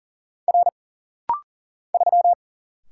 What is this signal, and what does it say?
Morse code, heard as a keyed tone, about 40 wpm, 700 Hz. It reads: R 3
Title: Single Letter-Number Alt - Rapid-Fire 40wpm